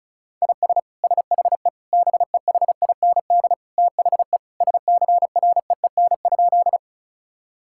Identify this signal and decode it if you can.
Morse code, heard as a keyed tone, about 34 wpm, 700 Hz. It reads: IS SHE BEHIND THE SCREEN?
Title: Sentences from Top 1000 Words - Review Part 6 - 35wpm